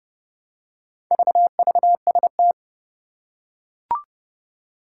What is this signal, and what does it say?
Morse code, heard as a keyed tone, about 30 wpm, 700 Hz. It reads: VVST
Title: Four Challenging Letter-Number Combinations 30wpm